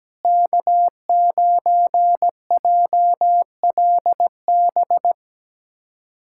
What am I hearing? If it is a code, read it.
Morse code, heard as a keyed tone, about 17 wpm, 700 Hz. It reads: K9JLB
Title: USA Callsign DE Callsign - 5 Characters - 17wpm